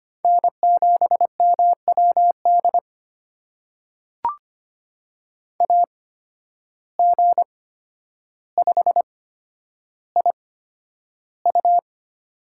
Morse code, heard as a keyed tone, about 25 wpm, spaced as for 11 wpm, 700 Hz. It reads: N7MWD AG5IU
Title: USA Call Signs - 5 Characters - 8x Character Spacing - 25wpm